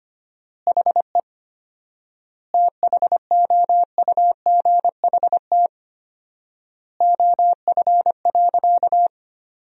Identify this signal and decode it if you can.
Morse code, heard as a keyed tone, about 25 wpm, 700 Hz. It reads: HE THOUGHT OF.
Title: 4x Word-Spacing - Sentences from Top 300 Words - 25wpm